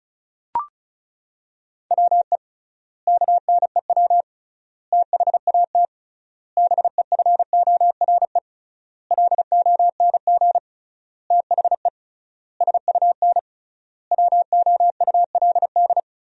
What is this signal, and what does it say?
Morse code, heard as a keyed tone, about 36 wpm, 700 Hz. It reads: WE KNEW THAT BEFORE LONG THE SUN WOULD
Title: 3x Word-Spacing - Sentences from Top 400 Words - 35wpm